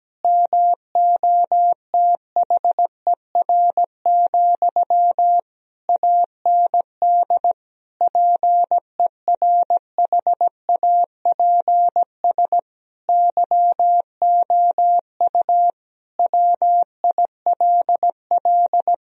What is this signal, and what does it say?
Morse code, heard as a keyed tone, about 17 wpm, 700 Hz. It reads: MOTHER, AND PERHAPS YOU WILL
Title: Sentences from Top 1000 Words - Review Part 1 - 17wpm